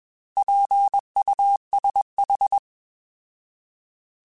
Morse code, heard as a keyed tone, about 21 wpm, 800 Hz. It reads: PUSH
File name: Alphabetical / Alphabetical mp3s / Ps @20